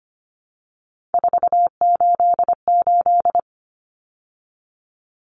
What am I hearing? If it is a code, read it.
Morse code, heard as a keyed tone, about 25 wpm, 700 Hz. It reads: 488
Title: Numbers - 3-Digit 25wpm